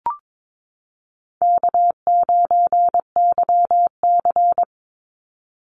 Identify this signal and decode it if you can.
Morse code, heard as a keyed tone, about 22 wpm, 700 Hz. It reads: K9YC
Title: Easy Call Signs Repeated 2x - 22wpm